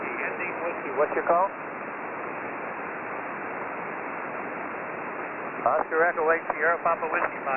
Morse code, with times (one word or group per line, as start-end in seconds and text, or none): none